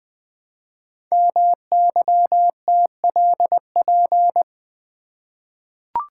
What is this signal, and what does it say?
Morse code, heard as a keyed tone, about 20 wpm, 700 Hz. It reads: MYTLP
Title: Code Groups - 5 Letters - 20wpm